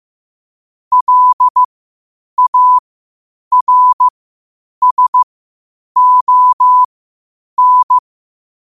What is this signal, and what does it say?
Morse code, heard as a keyed tone, about 15 wpm, spaced as for 8 wpm, 1000 Hz. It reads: LARSON
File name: qotdfiles / 8wpm